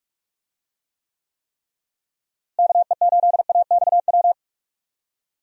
Morse code, heard as a keyed tone, about 45 wpm, 700 Hz. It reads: KE8AXW
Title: USA Call Signs - 6 Characters - Repeated 3x - 45wpm